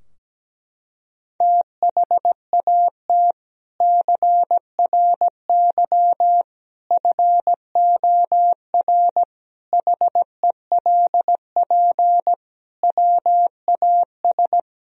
Morse code, heard as a keyed tone, about 17 wpm, 700 Hz. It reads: THAT CRY FOR HELP WAS
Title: Sentences from Top 1000 Words - Review Part 3 - 17wpm